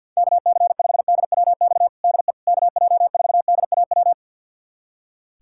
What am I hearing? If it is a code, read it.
Morse code, heard as a keyed tone, about 47 wpm, 700 Hz. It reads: KC5DWX DE KJ4DAW
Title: USA Callsign DE Callsign - 6 Characters - 50wpm